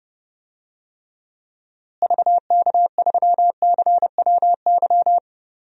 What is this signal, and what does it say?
Morse code, heard as a keyed tone, about 30 wpm, 700 Hz. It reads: VK3CWY
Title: DXCC-100 Call Signs - 6 Characters - Repeated 2x - 30wpm